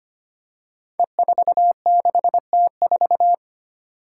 Morse code, heard as a keyed tone, about 25 wpm, 700 Hz. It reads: E46T4
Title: Five Challenging Letter-Number Combinations 25wpm